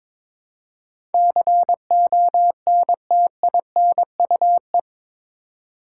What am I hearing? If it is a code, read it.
Morse code, heard as a keyed tone, about 22 wpm, 700 Hz. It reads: CONTINUE
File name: practice / 22wpm / Eight Letter Words - 22wpm